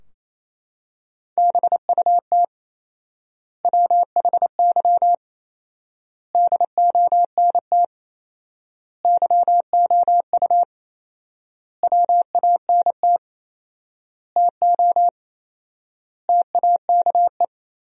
Morse code, heard as a keyed tone, about 27 wpm, 700 Hz. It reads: BUT WHY DONT YOU WANT TO TAKE
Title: 4x Word-Spacing - Sentences from Top 300 Words - 28wpm